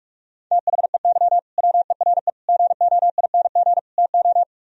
Morse code, heard as a keyed tone, about 46 wpm, 700 Hz. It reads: THEY WERE GOING TO